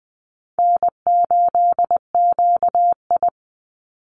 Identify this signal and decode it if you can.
Morse code, heard as a keyed tone, about 20 wpm, 700 Hz. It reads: N8QI
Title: USA Call Signs 20wpm